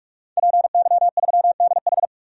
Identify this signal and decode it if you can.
Morse code, heard as a keyed tone, about 46 wpm, 700 Hz. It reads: PY3DH
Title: DXCC-100 Call Signs - 5 Characters - 45wpm